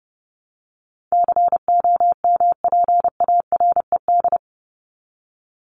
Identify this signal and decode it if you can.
Morse code, heard as a keyed tone, about 30 wpm, 700 Hz. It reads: COMPARED
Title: ED Suffix - Top 100 Words with Base Words 30wpm